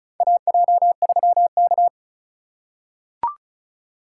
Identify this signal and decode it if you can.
Morse code, heard as a keyed tone, about 36 wpm, 700 Hz. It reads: AJ3K
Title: USA Call Signs - 4 Characters - 3x Character Spacing - 35wpm